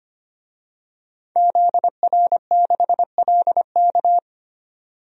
Morse code, heard as a keyed tone, about 25 wpm, 700 Hz. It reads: ZR6LK
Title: DXCC-100 Call Signs - 5 Characters - 25wpm